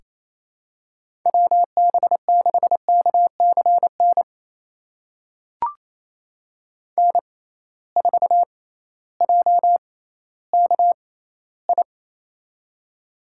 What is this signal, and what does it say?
Morse code, heard as a keyed tone, about 28 wpm, spaced as for 15 wpm, 700 Hz. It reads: WB6KCN N4JKI
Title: USA Call Signs - 6x Character Spacing - 28wpm